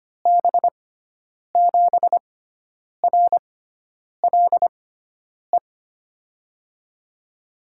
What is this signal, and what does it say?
Morse code, heard as a keyed tone, about 25 wpm, spaced as for 9 wpm, 700 Hz. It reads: B7RLE
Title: USA Call Signs - 6 Characters - 6x Character Spacing - 25wpm